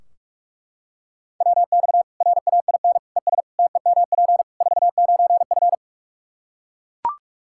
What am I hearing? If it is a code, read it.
Morse code, heard as a keyed tone, about 42 wpm, 700 Hz. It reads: WX RAIN ES TEMP 49F
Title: Practice QSO - Exchange 2 - Info - 45wpm